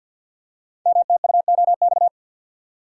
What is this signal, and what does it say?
Morse code, heard as a keyed tone, about 50 wpm, 700 Hz. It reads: MTUOX
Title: Code Groups - 5 Letters - 50wpm